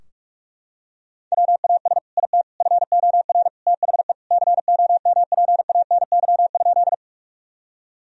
Morse code, heard as a keyed tone, about 44 wpm, 700 Hz. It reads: WAS IT FOR THE COMPANY?
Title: Sentences from Top 1000 Words - Review Part 3 - 45wpm